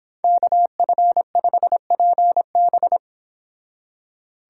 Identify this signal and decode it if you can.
Morse code, heard as a keyed tone, about 29 wpm, 700 Hz. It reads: KF5PB